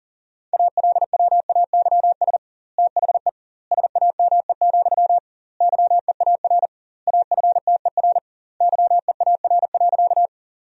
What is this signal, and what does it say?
Morse code, heard as a keyed tone, about 40 wpm, 700 Hz. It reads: ALWAYS THE SAME, YEAR AFTER YEAR.
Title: 2x Word-Spacing - Sentences from Top 300 Words - 40wpm